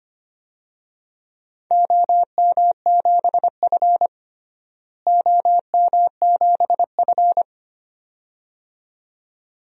Morse code, heard as a keyed tone, about 25 wpm, 700 Hz. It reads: OM7F OM7F
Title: DXCC-100 Call Signs - 4 Characters - Repeated 2x - 25wpm